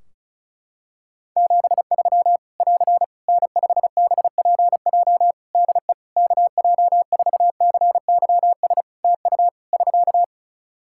Morse code, heard as a keyed tone, about 36 wpm, 700 Hz. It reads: 73 <AR> N5BPJ DE KJ4CYS TU <SK>
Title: Practice QSO - Exchange 4 - Ending - 35wpm